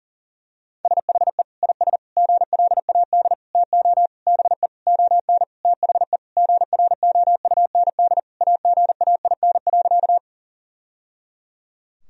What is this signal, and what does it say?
Morse code, heard as a keyed tone, about 40 wpm, 700 Hz. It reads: SHE IS GLAD TO BE ON THE GROUND AGAIN.
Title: Sentences from Top 1000 Words - Review Part 2 - 40wpm